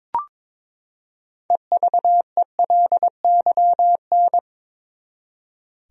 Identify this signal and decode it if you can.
Morse code, heard as a keyed tone, about 22 wpm, 700 Hz. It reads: EVELYN
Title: USA Common Names - 22wpm